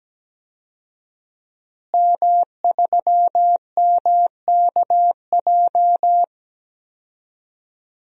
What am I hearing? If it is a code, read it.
Morse code, heard as a keyed tone, about 17 wpm, 700 Hz. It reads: M3MKJ
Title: DXCC-100 Call Signs - 5 Characters - Repeated 3x - 17wpm